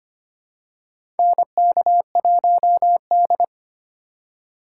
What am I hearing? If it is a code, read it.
Morse code, heard as a keyed tone, about 25 wpm, 700 Hz. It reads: NK1D
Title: USA Call Signs - 4 Characters - 2x Character Spacing - 25wpm